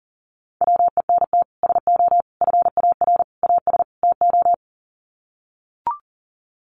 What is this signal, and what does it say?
Morse code, heard as a keyed tone, about 40 wpm, 700 Hz. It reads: WENT SO FAR AS TO
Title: Sets of 5 Words Encore 40wpm